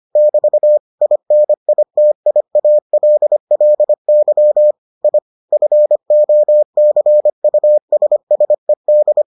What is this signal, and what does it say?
Morse code, heard as a keyed tone, about 25 wpm, 600 Hz. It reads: <BT> INITIALLY I FOCUSSED